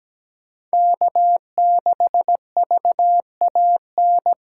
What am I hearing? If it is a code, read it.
Morse code, heard as a keyed tone, about 17 wpm, 700 Hz. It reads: K6VAN
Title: USA Call Signs - 5 Characters - 3x Character Spacing - 17wpm